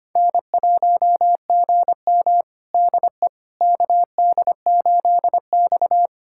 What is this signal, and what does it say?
Morse code, heard as a keyed tone, about 25 wpm, 700 Hz. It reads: N1GM DE KD8X
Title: USA Callsign DE Callsign - 4 Characters - 25wpm